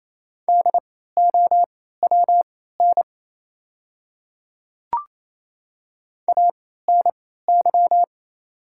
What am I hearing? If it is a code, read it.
Morse code, heard as a keyed tone, about 27 wpm, spaced as for 16 wpm, 690 Hz. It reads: DOWN ANY
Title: 3x Character Spacing - Top 200 Words - 28wpm